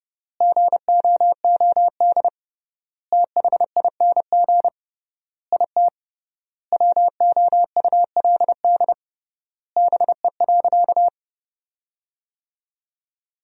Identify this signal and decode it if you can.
Morse code, heard as a keyed tone, about 30 wpm, 700 Hz. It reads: GOOD THING IT WOULD BE.
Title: 3x Word-Spacing - Sentences from Top 200 Words - 30wpm